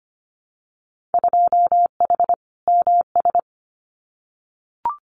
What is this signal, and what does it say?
Morse code, heard as a keyed tone, about 25 wpm, 700 Hz. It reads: 2H MS
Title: Winter Field Day Exchanges 25wpm